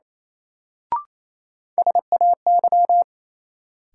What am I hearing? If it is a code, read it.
Morse code, heard as a keyed tone, about 28 wpm, 700 Hz. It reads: SAY
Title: Sets of 2 Words - New Word Drill - 28wpm